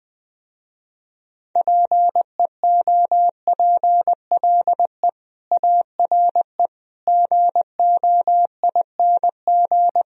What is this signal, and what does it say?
Morse code, heard as a keyed tone, about 20 wpm, 700 Hz. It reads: PEOPLE ARE GOING
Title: Speed-Racing - Sets of 4 Words 17wpm